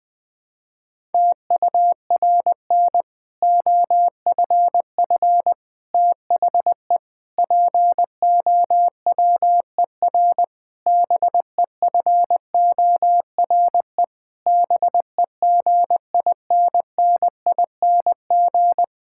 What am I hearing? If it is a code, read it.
Morse code, heard as a keyed tone, about 20 wpm, 700 Hz. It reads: TURN OFF THE POWER BEFORE BEGINNING
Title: Sentences from Top 1000 Words - Review Part 6 - 20wpm